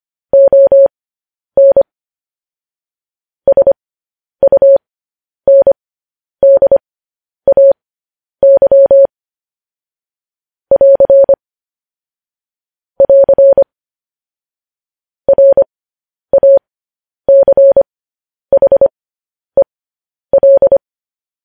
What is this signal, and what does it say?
Morse code, heard as a keyed tone, about 25 wpm, spaced as for 10 wpm, 550 Hz. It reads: ON SUNDAY <AR> <AR> RACHEL
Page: news-headlines-in-morse-code-at-10-wpm